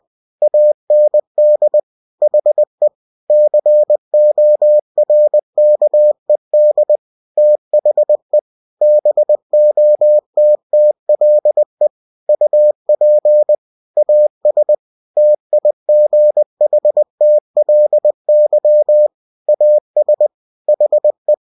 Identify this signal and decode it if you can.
Morse code, heard as a keyed tone, about 20 wpm, 600 Hz. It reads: AND HE CORKED THE BOTTLE UP AS TIGHTLY AS HE
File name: cw-books / winnie-the-pooh / winnie-20wpm-page-0092